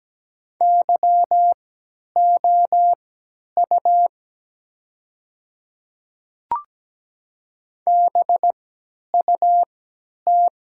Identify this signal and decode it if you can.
Morse code, heard as a keyed tone, about 17 wpm, spaced as for 10 wpm, 700 Hz. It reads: YOU BUT
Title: 3x Character Spacing - Top 100 Words - 17wpm